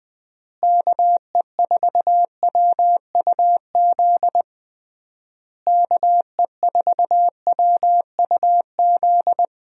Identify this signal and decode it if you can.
Morse code, heard as a keyed tone, about 20 wpm, 700 Hz. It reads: KE4WUZ KE4WUZ
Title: USA Call Signs - 6 Characters - Repeated 3x - 20wpm